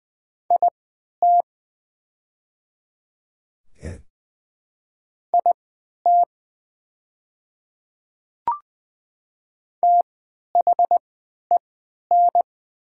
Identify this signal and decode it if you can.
Morse code, heard as a keyed tone, about 20 wpm, spaced as for 11 wpm, 700 Hz. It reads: IT IT THEN
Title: 3x Character Spacing - Top 100 Words - 20wpm